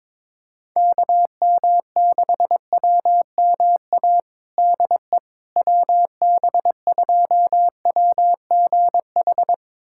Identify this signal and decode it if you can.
Morse code, heard as a keyed tone, about 22 wpm, 700 Hz. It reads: KM6WMA DE WB2WGH
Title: USA Callsign DE Callsign - 6 Characters - 22wpm